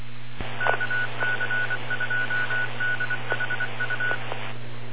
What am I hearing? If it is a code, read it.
Morse code, heard as a keyed tone, about 23 wpm, 1480 Hz, with noise. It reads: KC2DHU